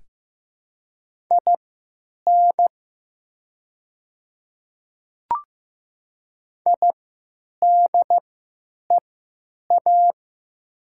Morse code, heard as a keyed tone, about 15 wpm, spaced as for 9 wpm, 700 Hz. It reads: IN IDEA
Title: 3x Character Spacing - Top 300 Words - 15wpm